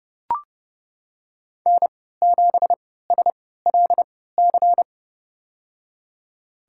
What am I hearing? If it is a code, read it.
Morse code, heard as a keyed tone, about 32 wpm, spaced as for 17 wpm, 700 Hz. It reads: N7SLC
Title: USA Call Signs - 5 Characters - 3x Character Spacing - 30wpm